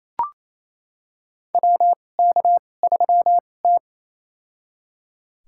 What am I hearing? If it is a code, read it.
Morse code, heard as a keyed tone, about 28 wpm, spaced as for 20 wpm, 700 Hz. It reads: WK3T
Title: USA Call Signs - 4 Characters - 2x Character Spacing - 28wpm